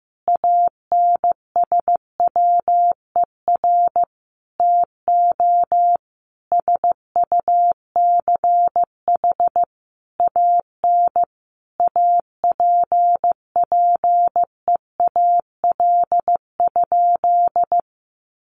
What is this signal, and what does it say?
Morse code, heard as a keyed tone, about 15 wpm, 700 Hz. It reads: ANSWER TO SUCH AN APPEAL?
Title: Sentences from Top 1300 Words 15wpm